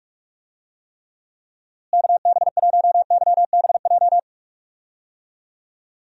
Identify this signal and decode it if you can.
Morse code, heard as a keyed tone, about 41 wpm, 700 Hz. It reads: KB1YBJ
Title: USA Call Signs - 6 Characters - Repeated 2x - 45wpm